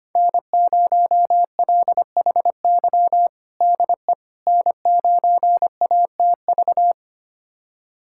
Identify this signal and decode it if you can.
Morse code, heard as a keyed tone, about 25 wpm, 700 Hz. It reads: N0LHY DE N9ATV
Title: USA Callsign DE Callsign - 5 Characters - 25wpm